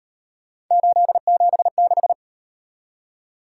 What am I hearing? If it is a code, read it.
Morse code, heard as a keyed tone, about 37 wpm, 700 Hz. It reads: 876